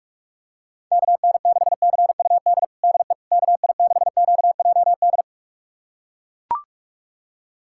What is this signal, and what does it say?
Morse code, heard as a keyed tone, about 42 wpm, 690 Hz. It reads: KN6CUD DE KI6QJD
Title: USA Callsign DE Callsign - 6 Characters - 45wpm